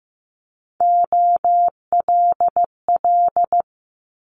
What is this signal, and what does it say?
Morse code, heard as a keyed tone, about 15 wpm, 700 Hz. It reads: OLL
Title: Multi-letter Phonemes Quick Fire 15wpm